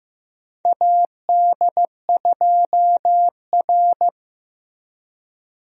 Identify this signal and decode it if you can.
Morse code, heard as a keyed tone, about 15 wpm, 700 Hz. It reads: AD2R
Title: USA Call Signs - 4 Characters - Repeated 3x - 15wpm